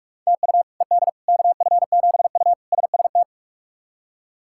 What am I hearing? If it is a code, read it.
Morse code, heard as a keyed tone, about 41 wpm, 700 Hz. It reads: TU ED KF7U SST
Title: K1USN SST QSO Practice - 45wpm